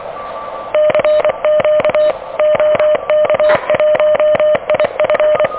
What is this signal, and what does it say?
Morse code, heard as a keyed tone, about 27 wpm, 600 Hz. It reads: CQ OK1IF
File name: OK QRP 2023 / OK QRP 2023